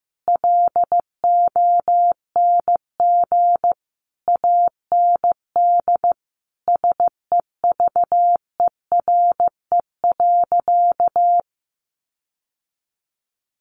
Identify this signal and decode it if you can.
Morse code, heard as a keyed tone, about 15 wpm, 700 Hz. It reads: LONG AND SEVERE.
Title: Sentences from Top 1400 Words 15wpm